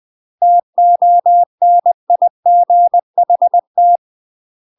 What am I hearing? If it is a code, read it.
Morse code, heard as a keyed tone, about 20 wpm, 700 Hz. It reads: TONIGHT
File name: UK MASTER QSOs TRAINING / 20WPM / UK MASTER QSOs 0013